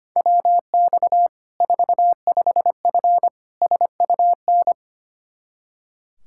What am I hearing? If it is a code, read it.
Morse code, heard as a keyed tone, about 25 wpm, 700 Hz. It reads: WX 45F SUN
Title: Practice QSO - Exchange 2 - Info - 25wpm